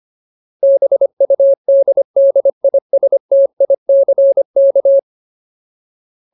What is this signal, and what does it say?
Morse code, heard as a keyed tone, about 25 wpm, 550 Hz. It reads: BUDDISTICK
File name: Ultimate Word List / 25WPM Ultimate